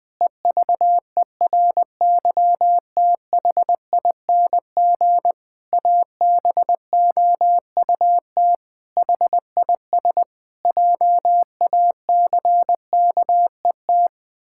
Morse code, heard as a keyed tone, about 20 wpm, 700 Hz. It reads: EVERYTHING ABOUT HIS JACKET
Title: Pangram Challenge v2 - 20wpm